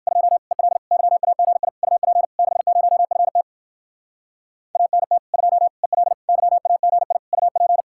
Morse code, heard as a keyed tone, about 59 wpm, 700 Hz. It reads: 2 EL YAGI UP 60FT ANT 2 EL YAGI UP